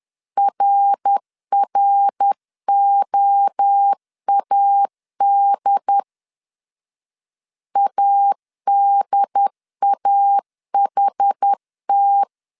Morse code, heard as a keyed tone, about 10 wpm, 800 Hz. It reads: RROAD ADAHT
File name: CW / DET / 1 3 1500C S C 10wpm 3335